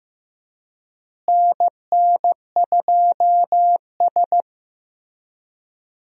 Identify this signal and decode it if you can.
Morse code, heard as a keyed tone, about 15 wpm, 700 Hz. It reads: NN2S
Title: USA Call Signs - 4 Characters - Repeated 3x - 15wpm